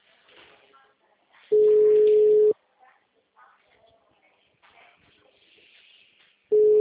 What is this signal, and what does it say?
Ringback tone (Europe, 425 Hz), with speech.